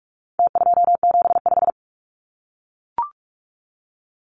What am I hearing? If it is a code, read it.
Morse code, heard as a keyed tone, about 44 wpm, 700 Hz. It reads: T275
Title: Numbers - Contest Style - 4-Digit 45wpm